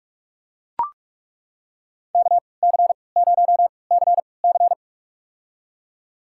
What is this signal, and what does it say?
Morse code, heard as a keyed tone, about 42 wpm, spaced as for 25 wpm, 700 Hz. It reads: KC0CC
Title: USA Call Signs - 3x Character Spacing - 45wpm